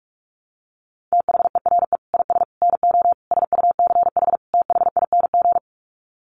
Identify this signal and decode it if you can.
Morse code, heard as a keyed tone, about 44 wpm, 700 Hz. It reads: THERE IS NO SUCH THING
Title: Sets of 5 Words Encore 45wpm